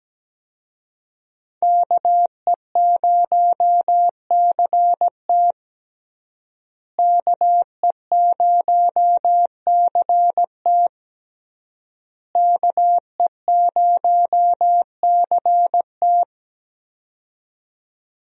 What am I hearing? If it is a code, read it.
Morse code, heard as a keyed tone, about 17 wpm, 700 Hz. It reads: KE0CT KE0CT KE0CT
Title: USA Call Signs - 5 Characters - Repeated 3x - 17wpm